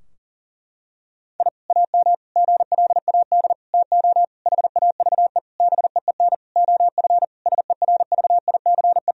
Morse code, heard as a keyed tone, about 40 wpm, 700 Hz. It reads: I AM GLAD TO HAVE BEEN OF SERVICE
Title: Sentences from Top 1000 Words - Review Part 4 - 40wpm